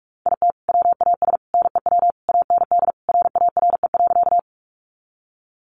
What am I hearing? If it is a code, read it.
Morse code, heard as a keyed tone, about 45 wpm, 700 Hz. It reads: IT WAS NEW AND RARE.
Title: Sentences from Top 1100 Words 45wpm